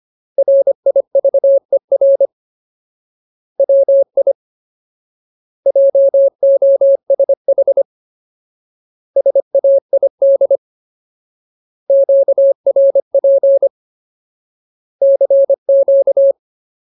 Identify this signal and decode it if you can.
Morse code, heard as a keyed tone, about 25 wpm, 550 Hz. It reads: RIVER WI JOSH SAID QRP CQ
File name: Ultimate Word List / 25WPM Ultimate